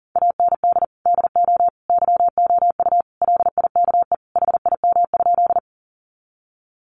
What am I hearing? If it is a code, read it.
Morse code, heard as a keyed tone, about 37 wpm, 700 Hz. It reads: AND DO YOU LIKE HIM?
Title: Sentences from Top 100 Words 40wpm